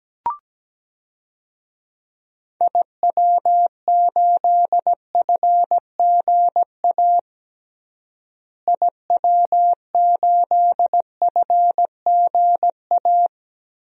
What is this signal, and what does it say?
Morse code, heard as a keyed tone, about 17 wpm, 700 Hz. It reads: IW8FGA IW8FGA
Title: DXCC-100 Call Signs - 6 Characters - Repeated 3x - 17wpm